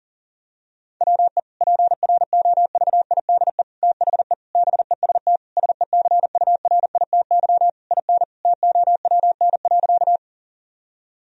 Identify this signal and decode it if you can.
Morse code, heard as a keyed tone, about 40 wpm, 700 Hz. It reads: WE PROVIDE THE BEST SECURITY IN TOWN.
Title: Sentences from Top 1000 Words - Review Part 4 - 40wpm